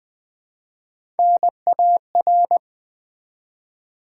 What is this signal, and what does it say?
Morse code, heard as a keyed tone, about 20 wpm, 700 Hz. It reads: NAR
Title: Build A Word - 3 Letters At A Time - 6-Letter Words - 20wpm